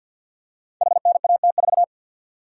Morse code, heard as a keyed tone, about 52 wpm, 700 Hz. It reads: HNAT4
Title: Code Groups - 5 Letter:Numbers - 50wpm